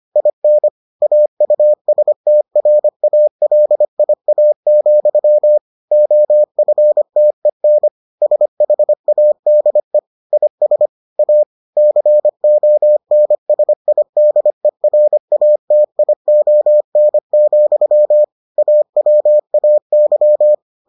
Morse code, heard as a keyed tone, about 25 wpm, 600 Hz. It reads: IN AUSTRALIA, OFTEN SHADE IS A CONSIDERATION, AWAY